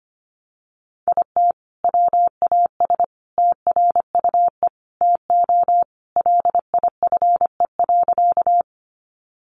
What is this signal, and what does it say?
Morse code, heard as a keyed tone, about 25 wpm, 700 Hz. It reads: IT WAS TRUE TO LIFE.
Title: Sentences from Top 400 Words 25wpm